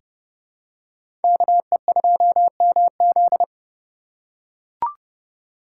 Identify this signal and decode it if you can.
Morse code, heard as a keyed tone, about 30 wpm, 700 Hz. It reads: KE2MZ
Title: USA Call Signs - 5 Characters - 3x Character Spacing - 30wpm